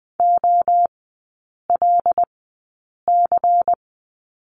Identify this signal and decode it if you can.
Morse code, heard as a keyed tone, about 20 wpm, spaced as for 8 wpm, 690 Hz. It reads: OLC
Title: 093 Lesson 16e - Letter c - Three Characters 20wpm